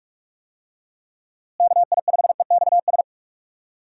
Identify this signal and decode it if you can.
Morse code, heard as a keyed tone, about 46 wpm, 700 Hz. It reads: KI5EXS
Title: USA Call Signs - 6 Characters - Repeated 3x - 45wpm